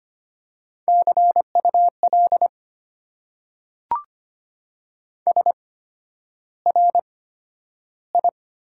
Morse code, heard as a keyed tone, about 25 wpm, spaced as for 11 wpm, 700 Hz. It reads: CUL SRI
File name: Top 90 QSO Elements - 8x Character Spacing - 25wpm